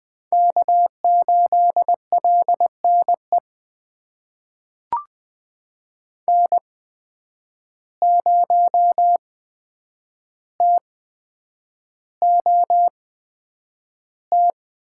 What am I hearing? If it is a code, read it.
Morse code, heard as a keyed tone, about 20 wpm, spaced as for 9 wpm, 700 Hz. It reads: K8LNE N0TOT
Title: USA Call Signs - 5 Characters - 8x Character Spacing - 20wpm